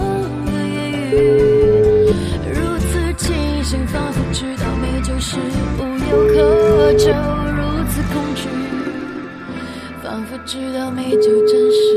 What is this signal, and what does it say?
Ringback tone (Europe, 425 Hz), with speech.